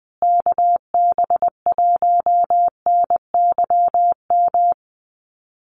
Morse code, heard as a keyed tone, about 20 wpm, 700 Hz. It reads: KB1NYM
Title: USA Call Signs Repeated 2x 20wpm